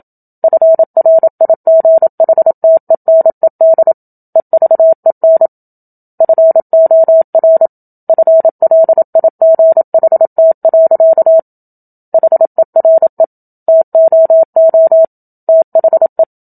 Morse code, heard as a keyed tone, about 27 wpm, 650 Hz. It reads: FRIGHTENED EVEN FOR FLIGHT. HERE TOO THE
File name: Book Two / 01-Chapter 1 - 27 WPM